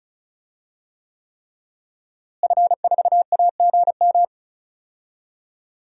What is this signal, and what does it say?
Morse code, heard as a keyed tone, about 36 wpm, 700 Hz. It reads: F4AGM
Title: DXCC-100 Call Signs - 5 Characters - Repeated 3x - 35wpm